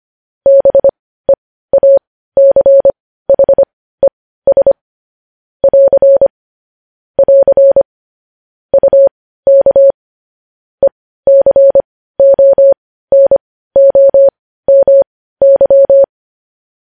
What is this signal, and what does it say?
Morse code, heard as a keyed tone, about 25 wpm, spaced as for 15 wpm, 550 Hz. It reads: BEACHES <AR> <AR> UK ECONOMY
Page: news-headlines-in-morse-code-at-15-wpm